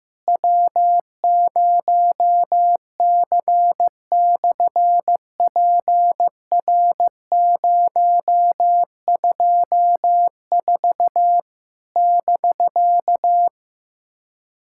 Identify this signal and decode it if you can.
Morse code, heard as a keyed tone, about 15 wpm, 700 Hz. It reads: W0C/PR024 <BK>
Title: SOTA QSO Practice 15wpm